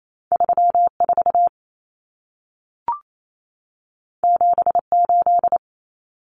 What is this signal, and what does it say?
Morse code, heard as a keyed tone, about 28 wpm, 700 Hz. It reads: 34 78
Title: Numbers - 2-Digit 28wpm